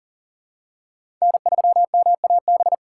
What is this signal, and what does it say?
Morse code, heard as a keyed tone, about 40 wpm, 700 Hz. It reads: N3MAB
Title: USA Call Signs - 5 Characters - 2x Character Spacing - 40wpm